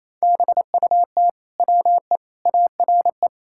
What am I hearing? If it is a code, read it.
Morse code, heard as a keyed tone, about 28 wpm, 700 Hz. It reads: BUT WE ARE
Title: Sentences from Top 1600 Words 28wpm